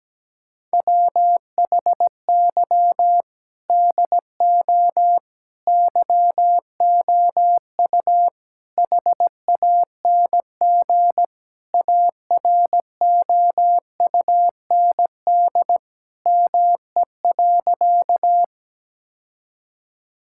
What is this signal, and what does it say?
Morse code, heard as a keyed tone, about 17 wpm, 700 Hz. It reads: WHY DO YOU HANG AROUND ME.
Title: Sentences from Top 1600 Words 17wpm